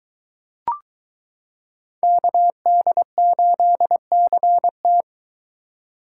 Morse code, heard as a keyed tone, about 22 wpm, 700 Hz. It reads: KD8CT